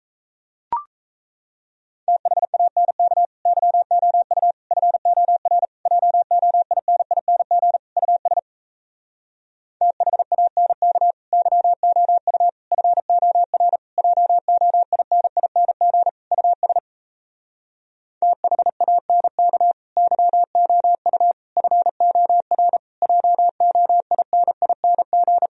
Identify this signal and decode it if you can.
Morse code, heard as a keyed tone, about 37 wpm, 700 Hz. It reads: THANK YOU FOR JOINING US THANK YOU FOR JOINING US THANK YOU FOR JOINING